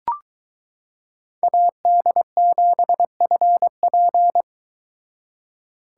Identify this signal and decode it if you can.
Morse code, heard as a keyed tone, about 23 wpm, 700 Hz. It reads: AD7FP